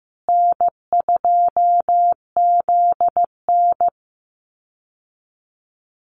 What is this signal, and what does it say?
Morse code, heard as a keyed tone, about 15 wpm, 700 Hz. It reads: N2ZN
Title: Easy Call Signs Repeated 3x - 15wpm